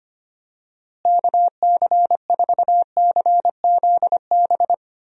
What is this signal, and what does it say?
Morse code, heard as a keyed tone, about 25 wpm, 700 Hz. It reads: KC4CZB
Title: USA Call Signs - 6 Characters - Repeated 2x - 25wpm